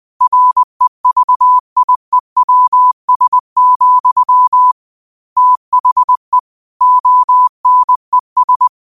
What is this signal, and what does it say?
Morse code, heard as a keyed tone, about 20 wpm, 1000 Hz. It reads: REVIEWS, THE ONES
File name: qotdfiles / 20wpm